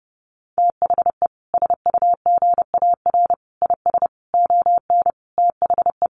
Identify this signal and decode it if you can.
Morse code, heard as a keyed tone, about 30 wpm, 700 Hz. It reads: THE SUGAR IS ON THE
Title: Sentences from Top 1200 Words 30wpm